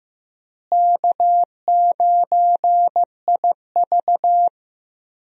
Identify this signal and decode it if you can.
Morse code, heard as a keyed tone, about 15 wpm, 700 Hz. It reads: K9IV
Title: USA Call Signs - 4 Characters - Repeated 3x - 15wpm